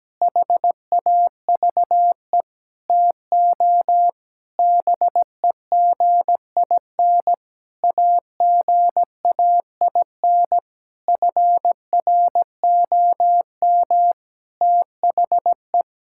Morse code, heard as a keyed tone, about 17 wpm, 700 Hz. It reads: HAVE TO BEGIN AGAIN FROM THE
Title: Sentences from Top 1000 Words - Review Part 5 - 17wpm